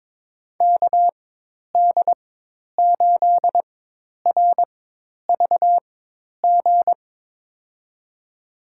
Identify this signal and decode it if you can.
Morse code, heard as a keyed tone, about 22 wpm, spaced as for 10 wpm, 700 Hz. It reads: KD8RVG
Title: USA Call Signs - 6 Characters - 4x Character Spacing - 22wpm